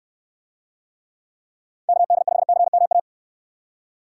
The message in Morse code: KD5CMA

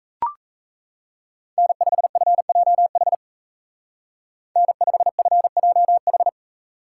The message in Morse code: N5FJH N5FJH

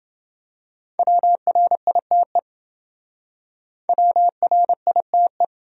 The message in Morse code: WRITE WRITE

